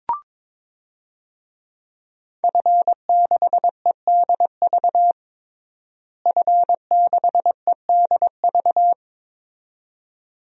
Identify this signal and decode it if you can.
Morse code, heard as a keyed tone, about 22 wpm, 700 Hz. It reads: F6EDV F6EDV
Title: DXCC-100 Call Signs - 5 Characters - Repeated 2x - 22wpm